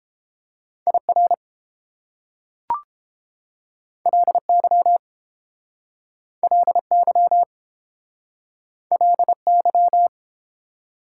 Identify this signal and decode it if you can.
Morse code, heard as a keyed tone, about 29 wpm, 700 Hz. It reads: IR LY LY LY